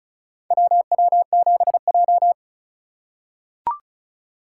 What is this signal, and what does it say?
Morse code, heard as a keyed tone, about 34 wpm, 700 Hz. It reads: WW7J
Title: USA Call Signs - 2x Character Spacing - 35wpm